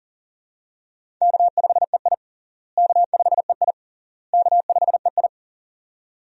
Morse code, heard as a keyed tone, about 40 wpm, 700 Hz. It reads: K5EI K5EI K5EI
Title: USA Call Signs - 4 Characters - Repeated 3x - 40wpm